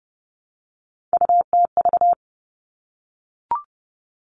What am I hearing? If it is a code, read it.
Morse code, heard as a keyed tone, about 30 wpm, 700 Hz. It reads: UTV
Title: All Cut Numbers - 3-Digit 30wpm